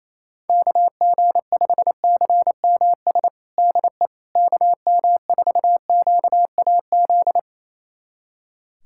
Morse code, heard as a keyed tone, about 27 wpm, 700 Hz. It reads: KG5CMS DE KM4QAZ
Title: USA Callsign DE Callsign - 6 Characters - 28wpm